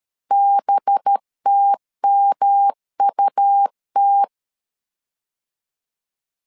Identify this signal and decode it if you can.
Morse code, heard as a keyed tone, about 13 wpm, 800 Hz. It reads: BTMUT
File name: CW / DET / 1 6 1500C S B 12wpm 3431